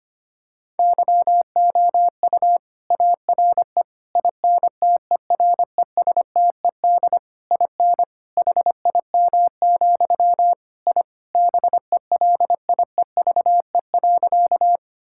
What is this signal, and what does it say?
Morse code, heard as a keyed tone, about 25 wpm, 700 Hz. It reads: YOU ARE INTERESTED IN HIM, I BELIEVE.
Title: Sentences from Top 1000 Words - Review Part 6 - 25wpm